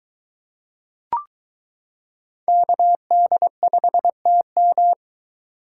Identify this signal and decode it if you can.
Morse code, heard as a keyed tone, about 22 wpm, 700 Hz. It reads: KD5TM